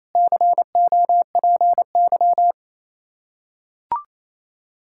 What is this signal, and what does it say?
Morse code, heard as a keyed tone, about 27 wpm, 700 Hz. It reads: COPY
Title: Practice QSO - Exchange 3 - More Info - 28wpm